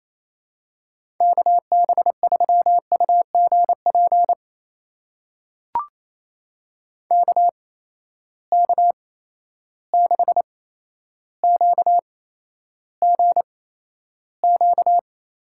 Morse code, heard as a keyed tone, about 28 wpm, spaced as for 12 wpm, 700 Hz. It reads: KB3UGP KK6QGQ
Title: USA Call Signs - 6 Characters - 8x Character Spacing - 28wpm